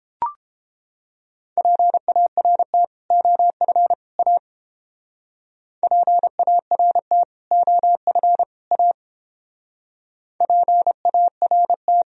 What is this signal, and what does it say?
Morse code, heard as a keyed tone, about 28 wpm, 700 Hz. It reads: PART OF A PART OF A PART